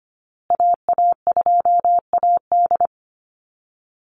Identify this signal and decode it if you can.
Morse code, heard as a keyed tone, about 25 wpm, 700 Hz. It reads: AA2AD
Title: Call Signs Repeated 2x - 25wpm